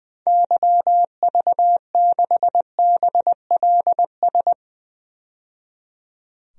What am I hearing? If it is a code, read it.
Morse code, heard as a keyed tone, about 20 wpm, 700 Hz. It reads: YV6BLS
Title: DXCC-100 Call Signs - 6 Characters - 20wpm